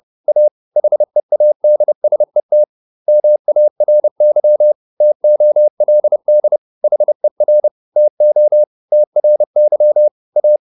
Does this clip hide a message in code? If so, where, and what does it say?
Yes: Morse code from 0.3 to 10.6 s, heard as a keyed tone, about 30 wpm, 600 Hz. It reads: A HEADSET MARY TOLD HER TO TRY A